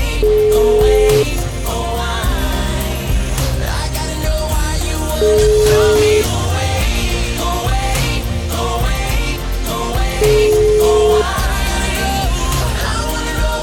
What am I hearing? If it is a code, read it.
Ringback tone (Europe, 425 Hz), with noise.